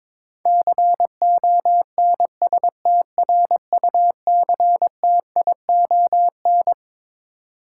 Morse code, heard as a keyed tone, about 22 wpm, 700 Hz. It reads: CONSTRUCTION
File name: Twelve Letter Words - 22wpm